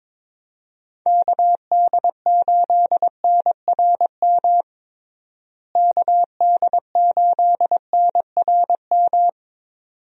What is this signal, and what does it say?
Morse code, heard as a keyed tone, about 22 wpm, 700 Hz. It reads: KD8NRM KD8NRM
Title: USA Call Signs - 6 Characters - Repeated 2x - 22wpm